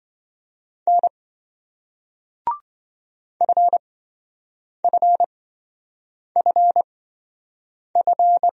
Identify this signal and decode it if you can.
Morse code, heard as a keyed tone, about 26 wpm, spaced as for 7 wpm, 700 Hz. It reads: N FFFF